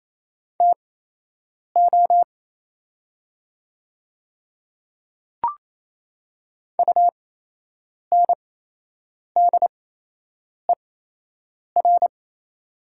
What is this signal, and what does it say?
Morse code, heard as a keyed tone, about 27 wpm, spaced as for 8 wpm, 690 Hz. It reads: TO UNDER
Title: 8x Character Spacing - Top 200 Words - 28wpm